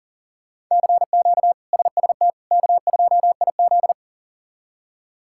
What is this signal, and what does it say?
Morse code, heard as a keyed tone, about 40 wpm, 700 Hz. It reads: CQ SST K2IZ
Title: K1USN SST QSO Practice - 40wpm